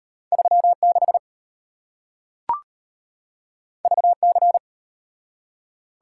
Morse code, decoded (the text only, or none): 36 VC